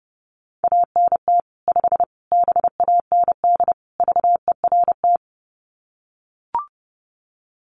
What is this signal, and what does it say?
Morse code, heard as a keyed tone, about 30 wpm, 700 Hz. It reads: ANT 5 BAND VERT
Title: Common QSO Phrases 30wpm